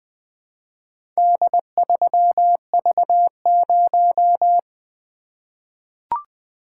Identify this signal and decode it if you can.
Morse code, heard as a keyed tone, about 20 wpm, 700 Hz. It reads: D3V0
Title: Four Letter-Number Combinations - 20wpm